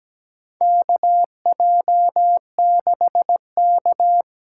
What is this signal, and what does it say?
Morse code, heard as a keyed tone, about 17 wpm, 700 Hz. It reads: KJ6K
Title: USA Call Signs - 6 Characters - 8x Character Spacing - 17wpm